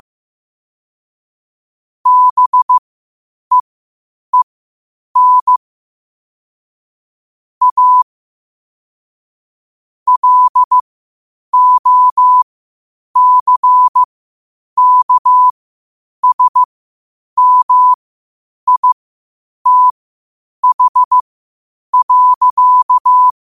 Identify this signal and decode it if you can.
Morse code, heard as a keyed tone, about 15 wpm, spaced as for 8 wpm, 1000 Hz. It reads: BEEN A LOCKSMITH.
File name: qotdfiles / 8wpm